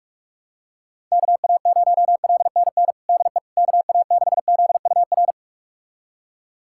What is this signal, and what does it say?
Morse code, heard as a keyed tone, about 46 wpm, 700 Hz. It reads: KA0LNN DE KA6ZUR
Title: USA Callsign DE Callsign - 6 Characters - 45wpm